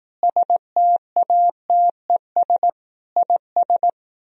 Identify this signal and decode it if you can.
Morse code, heard as a keyed tone, about 18 wpm, 700 Hz. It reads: STATES IS
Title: Speed-Racing - Sets of 4 Words 15wpm